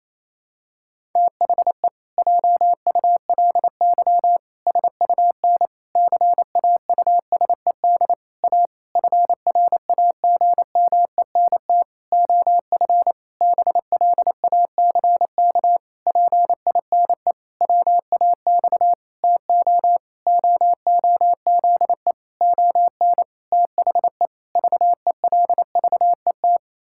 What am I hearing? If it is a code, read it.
Morse code, heard as a keyed tone, about 28 wpm, 700 Hz. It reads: THE JULY SUN CAUSED A FRAGMENT OF BLACK PINE WAX TO OOZE ON THE VELVET
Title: Pangram Challenge v2 - 28wpm